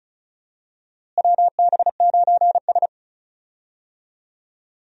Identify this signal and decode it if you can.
Morse code, heard as a keyed tone, about 34 wpm, 700 Hz. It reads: WB9S
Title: USA Call Signs - 4 Characters - Repeated 2x - 35wpm